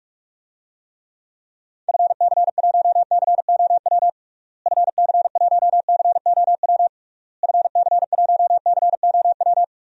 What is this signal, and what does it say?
Morse code, heard as a keyed tone, about 41 wpm, 700 Hz. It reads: FC1COW FC1COW FC1COW
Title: DXCC-100 Call Signs - 6 Characters - Repeated 3x - 45wpm